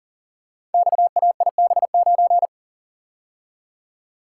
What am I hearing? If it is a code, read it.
Morse code, heard as a keyed tone, about 39 wpm, 700 Hz. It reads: XAIB9
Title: Code Groups - 5 Letter:Numbers - 40wpm